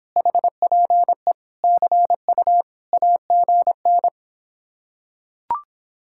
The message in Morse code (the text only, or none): HPE CU AGN